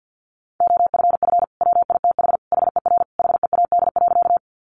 Morse code, heard as a keyed tone, about 50 wpm, 700 Hz. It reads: OFF WITH HER HEAD.
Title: Sentences from Top 300 Words 50wpm